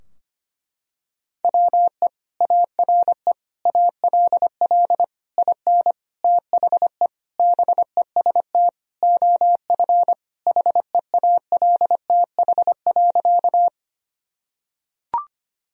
Morse code, heard as a keyed tone, about 25 wpm, 700 Hz. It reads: WE ARE ALL IN THE BEST OF HEALTH.
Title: Sentences from Top 1000 Words - Review Part 3 - 25wpm